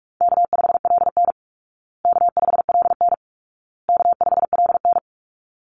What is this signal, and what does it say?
Morse code, heard as a keyed tone, about 44 wpm, 700 Hz. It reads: K5LN K5LN K5LN
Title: Easy Call Signs Repeated 3x - 45wpm